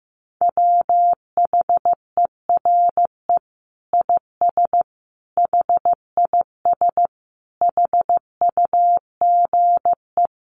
Morse code, heard as a keyed tone, about 15 wpm, 700 Hz. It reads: WHERE IS HIS HUGE
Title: Sentences from Top 1100 Words 15wpm